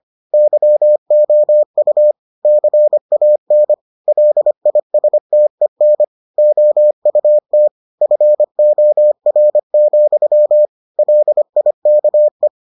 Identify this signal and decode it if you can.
Morse code, heard as a keyed tone, about 25 wpm, 600 Hz. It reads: YOU CAN LISTEN OUT FOR, LIKE